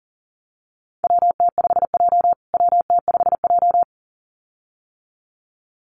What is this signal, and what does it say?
Morse code, heard as a keyed tone, about 40 wpm, 700 Hz. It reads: WT5J WT5J
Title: USA Call Signs Repeated 2x 40wpm